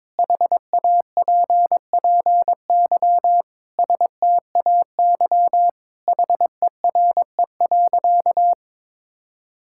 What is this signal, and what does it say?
Morse code, heard as a keyed tone, about 22 wpm, 700 Hz. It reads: HAPPY STAY HERE.
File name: practice / 22wpm / Sentences with 4-Word Phrases - 22wpm